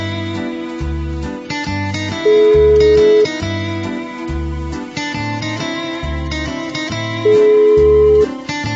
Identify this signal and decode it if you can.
Ringback tone (Europe, 425 Hz), with noise.